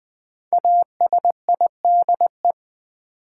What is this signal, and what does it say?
Morse code, heard as a keyed tone, about 20 wpm, 700 Hz. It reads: ASIDE
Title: Five Letter Words - 20wpm